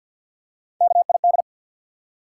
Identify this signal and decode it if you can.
Morse code, heard as a keyed tone, about 53 wpm, 700 Hz. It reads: KID
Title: Three Letter Words - 50wpm